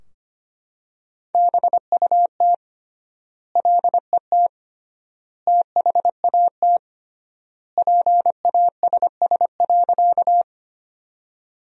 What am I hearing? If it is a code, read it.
Morse code, heard as a keyed tone, about 25 wpm, 700 Hz. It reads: BUT LET THAT PASS.
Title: 3x Word-Spacing - Sentences from Top 400 Words - 25wpm